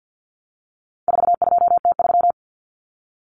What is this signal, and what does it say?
Morse code, heard as a keyed tone, about 48 wpm, 700 Hz. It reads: 42T3
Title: Common Cut Numbers - 4-Digit 50wpm